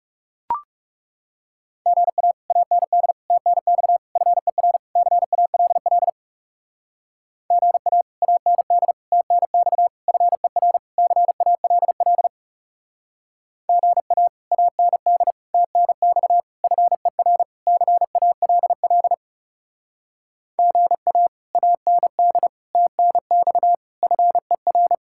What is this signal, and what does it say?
Morse code, heard as a keyed tone, about 39 wpm, 700 Hz. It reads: GA AND TNX FER CALL GA AND TNX FER CALL GA AND TNX FER CALL GA AND TNX FER